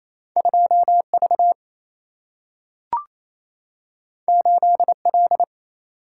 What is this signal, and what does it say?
Morse code, heard as a keyed tone, about 28 wpm, 700 Hz. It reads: 2V 8L
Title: Single-Letter Single-Number 28wpm